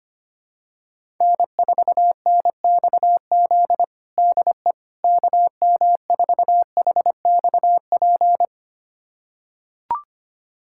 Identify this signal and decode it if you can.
Morse code, heard as a keyed tone, about 25 wpm, 700 Hz. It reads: N4NXZ DE KM4HXP
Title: USA Callsign DE Callsign 25wpm